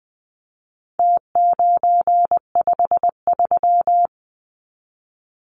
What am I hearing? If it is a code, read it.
Morse code, heard as a keyed tone, about 20 wpm, 700 Hz. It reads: T953
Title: Numbers - Contest Style - 4-Digit 20wpm